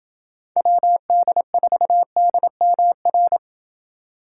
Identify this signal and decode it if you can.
Morse code, heard as a keyed tone, about 28 wpm, 700 Hz. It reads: WD4DMR